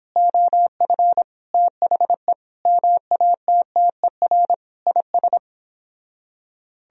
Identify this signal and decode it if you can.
Morse code, heard as a keyed tone, about 31 wpm, 700 Hz. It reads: OF THE MATTER IS